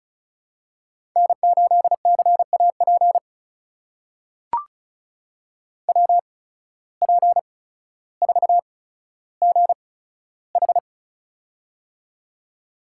Morse code, heard as a keyed tone, about 36 wpm, spaced as for 15 wpm, 700 Hz. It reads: N8CAP WP4GH
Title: USA Call Signs - 5 Characters - 8x Character Spacing - 35wpm